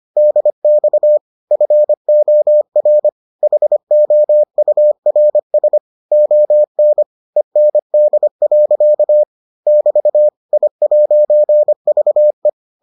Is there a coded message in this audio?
Yes: Morse code from 0.2 to 12.5 s, heard as a keyed tone, about 25 wpm, 600 Hz. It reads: DX FOR HOURS ON END. <BT> I'VE